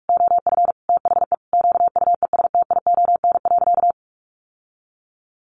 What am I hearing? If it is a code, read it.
Morse code, heard as a keyed tone, about 44 wpm, 700 Hz. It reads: OF THE QUESTION.
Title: Sentences from Top 600 Words 45wpm